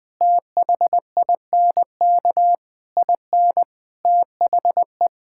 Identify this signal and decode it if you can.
Morse code, heard as a keyed tone, about 20 wpm, 700 Hz. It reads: THINK IN THE
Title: Sentences with 9-Word Phrases - 20wpm